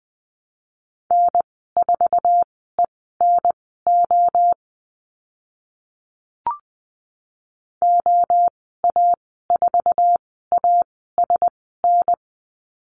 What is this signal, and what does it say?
Morse code, heard as a keyed tone, about 20 wpm, spaced as for 14 wpm, 700 Hz. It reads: N4ENO OA4ASN
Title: 062 Lesson 11g - Letter h - Call Signs 20wpm